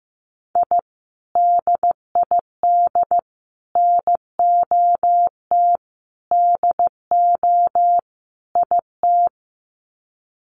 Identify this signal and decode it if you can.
Morse code, heard as a keyed tone, about 15 wpm, 700 Hz. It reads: I DID NOT DO IT
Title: Sets of 5 Words Encore2 15wpm